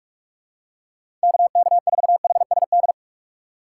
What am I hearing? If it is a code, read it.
Morse code, heard as a keyed tone, about 46 wpm, 700 Hz. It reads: KK4HSD
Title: USA Call Signs - 6 Characters - 2x Character Spacing - 45wpm